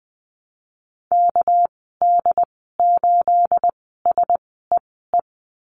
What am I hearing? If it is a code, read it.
Morse code, heard as a keyed tone, about 20 wpm, spaced as for 15 wpm, 700 Hz. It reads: KD8SEE
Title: 219 Lesson 35g - Number 0 - Call Signs 20wpm